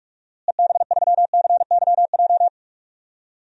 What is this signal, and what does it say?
Morse code, heard as a keyed tone, about 42 wpm, 700 Hz. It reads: EB3CYJ
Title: DXCC-100 Call Signs - 6 Characters - Repeated 2x - 45wpm